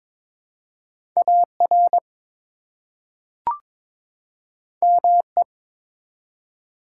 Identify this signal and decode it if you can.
Morse code, heard as a keyed tone, about 22 wpm, 700 Hz. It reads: AR ME
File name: practice / 22wpm / ARRL Sections v2 - 22wpm